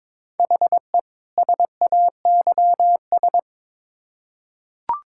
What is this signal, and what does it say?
Morse code, heard as a keyed tone, about 22 wpm, 700 Hz. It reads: HE SAYS
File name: Build A Phrase - 2-Word Phrases - 22wpm